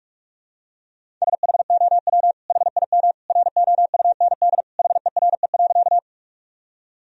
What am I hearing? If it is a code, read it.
Morse code, heard as a keyed tone, about 41 wpm, 700 Hz. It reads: SHOW HIM ROUND HERE.
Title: Sentences from Top 1000 Words - Review Part 1 - 45wpm